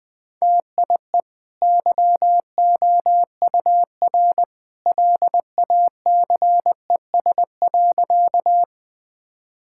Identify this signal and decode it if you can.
Morse code, heard as a keyed tone, about 21 wpm, 700 Hz. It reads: TIE YOUR LACES.
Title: Easy Sentences with Hints - 20wpm